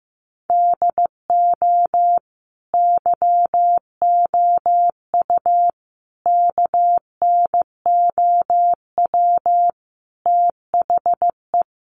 Morse code, heard as a keyed tone, about 15 wpm, 700 Hz. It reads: DO YOU KNOW THE
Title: Sentences from Top 1400 Words 15wpm